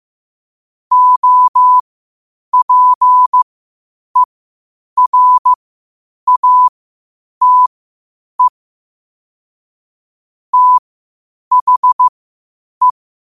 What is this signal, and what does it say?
Morse code, heard as a keyed tone, about 15 wpm, spaced as for 8 wpm, 1000 Hz. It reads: OPERATE THE
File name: qotdfiles / 8wpm